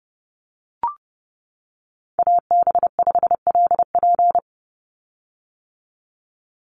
Morse code, heard as a keyed tone, about 30 wpm, 700 Hz. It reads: AB5LP
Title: USA Call Signs 30wpm